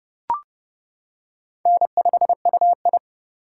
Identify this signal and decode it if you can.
Morse code, heard as a keyed tone, about 30 wpm, 700 Hz. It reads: N5UI
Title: K1USN SST QSO Practice - 30wpm